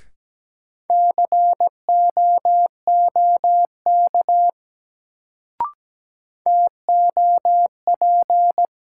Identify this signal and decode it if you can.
Morse code, heard as a keyed tone, about 17 wpm, 700 Hz. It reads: COOK TOP
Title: Sentences from Top 700 Words - New Word Drill - 17wpm